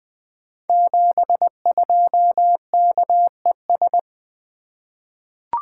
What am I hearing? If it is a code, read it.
Morse code, heard as a keyed tone, about 20 wpm, 700 Hz. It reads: 72KES
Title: Code Groups - 5 Letter:Numbers - 20wpm